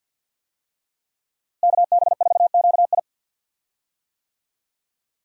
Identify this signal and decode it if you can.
Morse code, heard as a keyed tone, about 50 wpm, 700 Hz. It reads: KB4QI
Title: USA Call Signs - 5 Characters - Repeated 3x - 50wpm